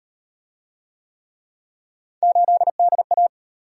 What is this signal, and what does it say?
Morse code, heard as a keyed tone, about 38 wpm, 700 Hz. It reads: 8DA